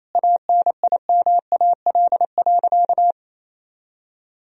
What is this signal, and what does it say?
Morse code, heard as a keyed tone, about 28 wpm, 700 Hz. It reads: ANIMAL.
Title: Sentences from Top 1000 Words - Review Part 1 - 28wpm